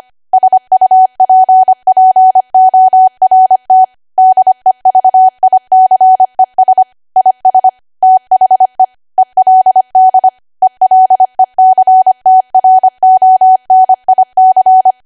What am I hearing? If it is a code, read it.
Morse code, heard as a keyed tone, about 25 wpm, 750 Hz. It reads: SUPPORT DEVICES IS THE ELD ELECTRONIC
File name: Archive / 25 WPM / 25WPM